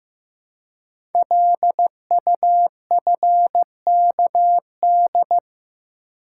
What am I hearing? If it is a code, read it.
Morse code, heard as a keyed tone, about 15 wpm, 700 Hz. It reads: LUFKD
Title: Code Groups - 5 Letters - 15wpm